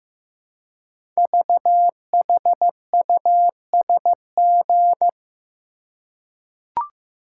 Morse code, heard as a keyed tone, about 15 wpm, 700 Hz. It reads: VHUSG
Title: Code Groups - 5 Letters - 15wpm